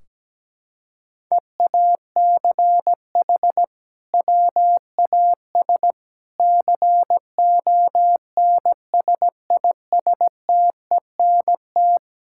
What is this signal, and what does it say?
Morse code, heard as a keyed tone, about 17 wpm, 700 Hz. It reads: EACH WAS CONSISTENT
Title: Sentences from Top 1600 Words 17wpm